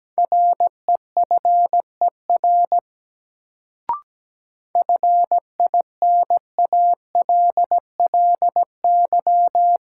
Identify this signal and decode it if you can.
Morse code, heard as a keyed tone, about 17 wpm, 700 Hz. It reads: REFER FINALLY
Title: Sentences from Top 1000 Words - New Word Drill - 17wpm